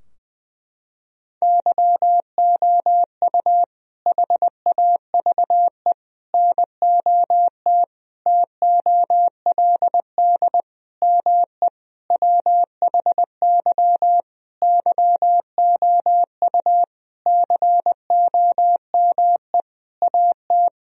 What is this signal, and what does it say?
Morse code, heard as a keyed tone, about 20 wpm, 700 Hz. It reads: YOU HAVE NOT TOLD ME WHY YOU COME AT
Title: Sentences from Top 1000 Words - Review Part 2 - 20wpm